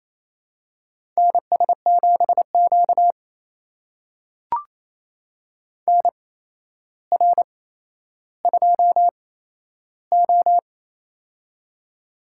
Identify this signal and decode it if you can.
Morse code, heard as a keyed tone, about 28 wpm, spaced as for 12 wpm, 700 Hz. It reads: NS7Q NR2O
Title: USA Call Signs - 4 Characters - 8x Character Spacing - 28wpm